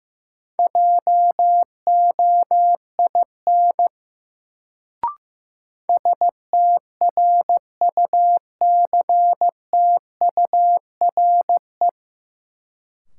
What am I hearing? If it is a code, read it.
Morse code, heard as a keyed tone, about 15 wpm, 700 Hz. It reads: JOIN STRUCTURE
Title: Sentences from Top 1000 Words - Review Part 6 - New Word Drill - 15wpm